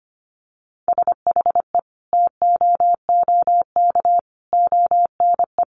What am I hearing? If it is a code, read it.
Morse code, heard as a keyed tone, about 25 wpm, 700 Hz. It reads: SHE TOOK ONE
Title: Sentences from Top 600 Words 25wpm